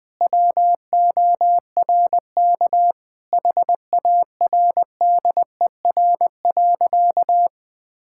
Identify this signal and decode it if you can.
Morse code, heard as a keyed tone, about 20 wpm, 700 Hz. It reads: WORK HARDER.
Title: Sentences with 6-Word Phrases - 20wpm